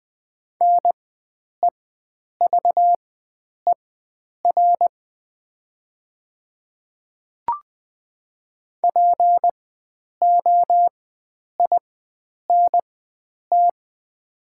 Morse code, heard as a keyed tone, about 20 wpm, spaced as for 9 wpm, 700 Hz. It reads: NEVER POINT
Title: 4x Character Spacing - Top 300 Words - 20wpm